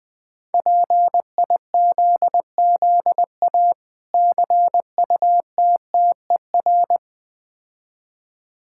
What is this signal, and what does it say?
Morse code, heard as a keyed tone, about 20 wpm, 690 Hz. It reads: PIZZA CUTTER
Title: Difficult Words with Hints - 20wpm